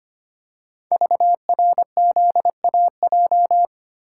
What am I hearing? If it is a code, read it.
Morse code, heard as a keyed tone, about 25 wpm, 700 Hz. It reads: VRZAJ
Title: Code Groups - 5 Letters - 25wpm